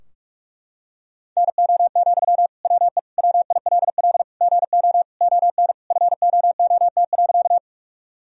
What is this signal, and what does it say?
Morse code, heard as a keyed tone, about 44 wpm, 700 Hz. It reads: NO, WE WILL GO ON FOOT.
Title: Sentences from Top 1000 Words - Review Part 3 - 45wpm